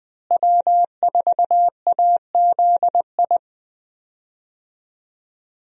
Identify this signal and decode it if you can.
Morse code, heard as a keyed tone, about 20 wpm, 700 Hz. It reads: W4AZI
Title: USA Call Signs - 5 Characters - Repeated 3x - 20wpm